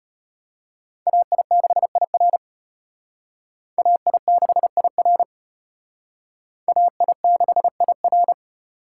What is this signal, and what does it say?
Morse code, heard as a keyed tone, about 32 wpm, 700 Hz. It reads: AI6IR AI6IR AI6IR